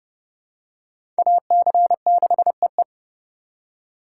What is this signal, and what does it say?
Morse code, heard as a keyed tone, about 30 wpm, 700 Hz. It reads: AC6EE
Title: USA Call Signs - 5 Characters - 4x Character Spacing - 30wpm